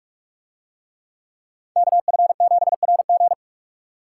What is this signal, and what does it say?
Morse code, heard as a keyed tone, about 42 wpm, 700 Hz. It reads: KF7RG